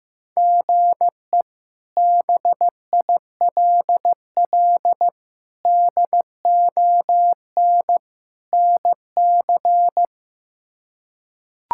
Morse code, heard as a keyed tone, about 15 wpm, 700 Hz. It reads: GE BILL DON NC
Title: K1USN SST QSO Practice - 15wpm